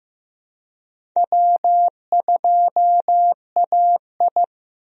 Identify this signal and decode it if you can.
Morse code, heard as a keyed tone, about 15 wpm, 700 Hz. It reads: W2AI
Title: USA Call Signs - 4 Characters - Repeated 3x - 15wpm